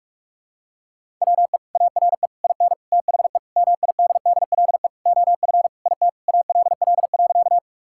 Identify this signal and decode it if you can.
Morse code, heard as a keyed tone, about 41 wpm, 700 Hz. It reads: WE ARE IN THE MIDDLE OF IT ALL.
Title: Sentences from Top 1000 Words - Review Part 6 - 45wpm